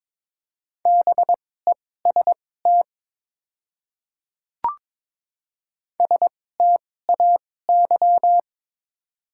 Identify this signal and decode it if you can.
Morse code, heard as a keyed tone, about 21 wpm, spaced as for 16 wpm, 690 Hz. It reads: BEST STAY
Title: 2x Character Spacing - Top 500 Words - 22wpm